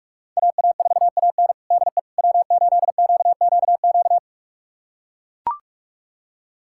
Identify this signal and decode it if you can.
Morse code, heard as a keyed tone, about 43 wpm, 700 Hz. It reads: AA4AN DE W8QQQ
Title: USA Callsign DE Callsign 45wpm